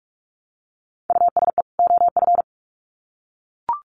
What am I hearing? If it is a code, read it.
Morse code, heard as a keyed tone, about 44 wpm, 700 Hz. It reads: USE OF
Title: Sets of 2 Words 45wpm